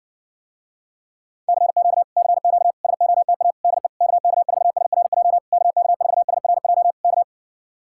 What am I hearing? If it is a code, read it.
Morse code, heard as a keyed tone, about 60 wpm, 700 Hz. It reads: CQ CQ SOTA DE KK4SRJ KK4SRJ K